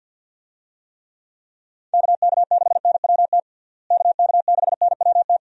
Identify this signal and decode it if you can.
Morse code, heard as a keyed tone, about 51 wpm, 700 Hz. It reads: KK6NWT KK6NWT
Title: USA Call Signs - 6 Characters - Repeated 2x - 50wpm